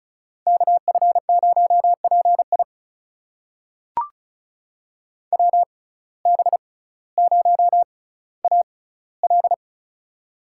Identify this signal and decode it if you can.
Morse code, heard as a keyed tone, about 36 wpm, spaced as for 18 wpm, 700 Hz. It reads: KF0PI WB0AL
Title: USA Call Signs - 5 Characters - 6x Character Spacing - 35wpm